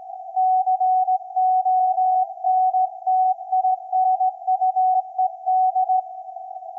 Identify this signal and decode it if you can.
Morse code, heard as a keyed tone, about 17 wpm, 750 Hz. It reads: CONTINUED
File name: iz7slz / VLF / SAQ 24-12-2024